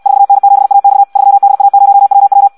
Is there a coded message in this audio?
Yes: Morse code from 0.1 to 2.5 s, heard as a keyed tone, about 74 wpm, 800 Hz. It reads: 4N7AX 3DA0WW